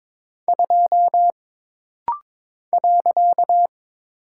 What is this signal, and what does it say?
Morse code, heard as a keyed tone, about 23 wpm, 700 Hz. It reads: 2 .
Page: This rapid-fire timing is geared to push your ICR (Instant Character Recognition) skills.